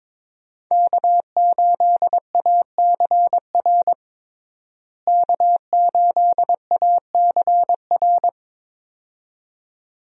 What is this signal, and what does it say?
Morse code, heard as a keyed tone, about 21 wpm, 700 Hz. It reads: K8ACR K8ACR
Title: USA Call Signs - 5 Characters - Repeated 2x - 22wpm